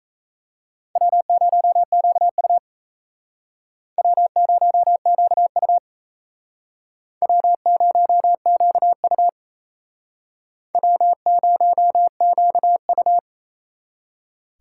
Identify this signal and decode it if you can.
Morse code, heard as a keyed tone, about 38 wpm, 700 Hz. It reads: W0QU W0QU W0QU W0QU